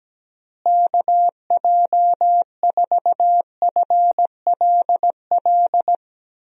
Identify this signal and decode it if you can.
Morse code, heard as a keyed tone, about 17 wpm, 700 Hz. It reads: KJ4FLL
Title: USA Call Signs - 6 Characters - Repeated 3x - 17wpm